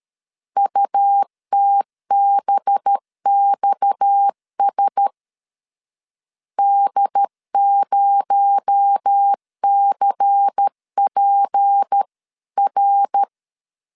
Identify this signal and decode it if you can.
Morse code, heard as a keyed tone, about 13 wpm, 800 Hz. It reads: UTBXS D0CPR
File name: CW / DET / 1 6 1500C S A 12wpm 3440